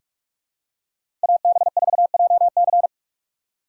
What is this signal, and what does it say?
Morse code, heard as a keyed tone, about 47 wpm, 700 Hz. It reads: AB4JC
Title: USA Call Signs - 5 Characters - 6x Character Spacing - 45wpm